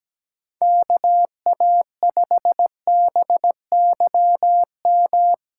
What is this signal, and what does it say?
Morse code, heard as a keyed tone, about 17 wpm, 700 Hz. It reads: KA5BYM
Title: USA Callsign DE Callsign 17wpm